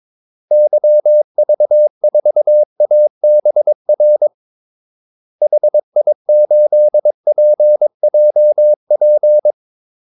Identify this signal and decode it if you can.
Morse code, heard as a keyed tone, about 22 wpm, 600 Hz. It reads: YV4ABR HI8PJP
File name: Callsigns-22wpm